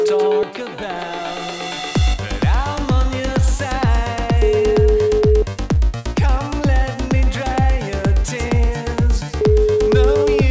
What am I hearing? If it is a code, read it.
Ringback tone (Europe, 425 Hz), with noise.